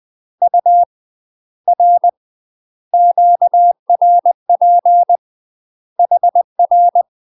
Morse code, heard as a keyed tone, about 20 wpm, 700 Hz. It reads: U R QRP HR
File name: UK MASTER QSOs TRAINING/20WPM